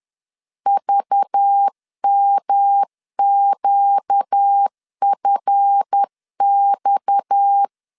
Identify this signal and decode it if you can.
Morse code, heard as a keyed tone, about 10 wpm, 800 Hz. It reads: VMQFX
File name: LEZIONI / 8 YJ45 / 8 YJ 10wpm 2651